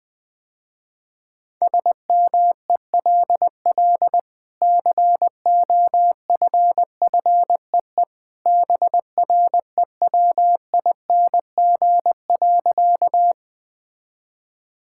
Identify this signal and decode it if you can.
Morse code, heard as a keyed tone, about 20 wpm, 700 Hz. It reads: SMELL COFFEE BREWING.
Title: Easy Sentences with Hints - 20wpm